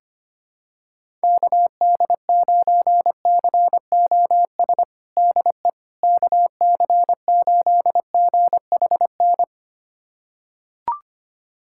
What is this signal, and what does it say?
Morse code, heard as a keyed tone, about 25 wpm, 700 Hz. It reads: KD9COS DE KC8GHN
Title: USA Callsign DE Callsign - 6 Characters - 25wpm